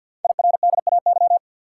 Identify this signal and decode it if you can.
Morse code, heard as a keyed tone, about 50 wpm, 690 Hz. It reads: IRDAY
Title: Code Groups - 5 Letters - 50wpm